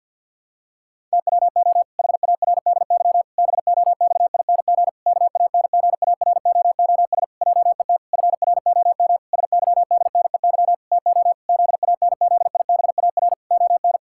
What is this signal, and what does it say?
Morse code, heard as a keyed tone, about 51 wpm, 700 Hz. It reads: TWO HARDY BOXING KANGAROOS JET FROM SYDNEY TO ZANZIBAR ON
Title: Pangram Challenge v2 - 50wpm